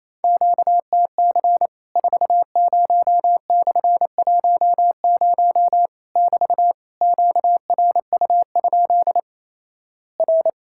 Morse code, heard as a keyed tone, about 28 wpm, 700 Hz. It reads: QTC 40/10 <BT> QRU? R
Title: Worked All Europe DX Contest - QTC Exchange - 28wpm